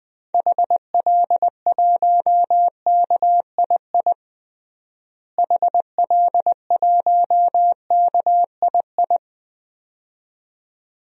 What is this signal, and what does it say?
Morse code, heard as a keyed tone, about 21 wpm, 700 Hz. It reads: HL1KII HL1KII
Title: DXCC-100 Call Signs - 6 Characters - Repeated 2x - 20wpm